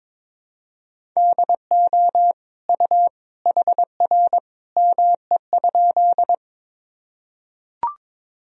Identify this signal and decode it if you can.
Morse code, heard as a keyed tone, about 22 wpm, 700 Hz. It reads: DO U HR ME?
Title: Sentences from Top 300 Words and Abbreviations 22wpm